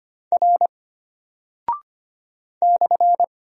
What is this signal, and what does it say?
Morse code, heard as a keyed tone, about 25 wpm, 700 Hz. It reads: R /
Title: Single Character (Letter:Number:Punctuation:Prosign) Alt 25wpm